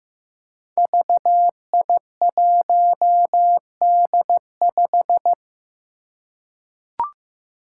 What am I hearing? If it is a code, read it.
Morse code, heard as a keyed tone, about 15 wpm, 700 Hz. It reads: VI1D5
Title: Five Challenging Letter-Number Combinations 15wpm